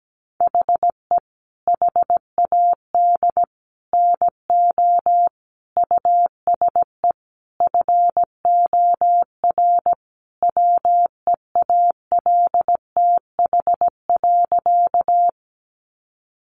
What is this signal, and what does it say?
Morse code, heard as a keyed tone, about 17 wpm, 700 Hz. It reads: HE HAD NO USE FOR WEALTH.
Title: Sentences from Top 1200 Words 17wpm